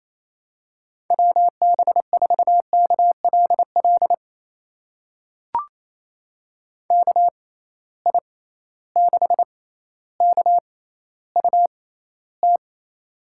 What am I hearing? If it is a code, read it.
Morse code, heard as a keyed tone, about 28 wpm, spaced as for 14 wpm, 700 Hz. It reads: WB4KLL KI6KUT
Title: USA Call Signs - 6 Characters - 6x Character Spacing - 28wpm